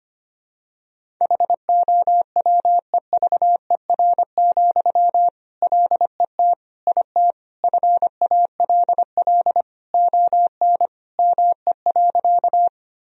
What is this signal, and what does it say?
Morse code, heard as a keyed tone, about 25 wpm, 700 Hz. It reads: HOWEVER, LET IT FALL ON ME.
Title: Sentences from Top 1000 Words - Review Part 2 - 25wpm